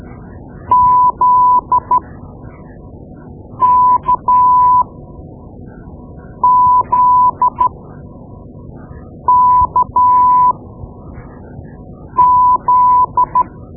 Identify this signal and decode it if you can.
Morse code, heard as a keyed tone, about 13 wpm, spaced as for 5 wpm, 1000 Hz. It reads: ZKZKZ
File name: Morse Lessons / Morse Lessons / Tape-2B